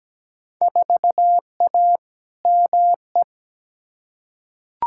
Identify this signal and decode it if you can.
Morse code, heard as a keyed tone, about 17 wpm, 700 Hz. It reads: 4A ME
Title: Field Day Exchanges - 17wpm